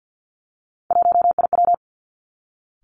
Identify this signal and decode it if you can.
Morse code, heard as a keyed tone, about 50 wpm, 700 Hz. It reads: 1IW
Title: Three Challenging Letter-Number Combinations 50wpm